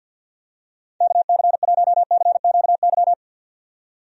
Morse code, heard as a keyed tone, about 47 wpm, 700 Hz. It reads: KC1CQY
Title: USA Call Signs - 6 Characters - 50wpm